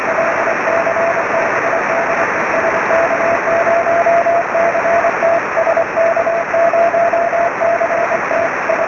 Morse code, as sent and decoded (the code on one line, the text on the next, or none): none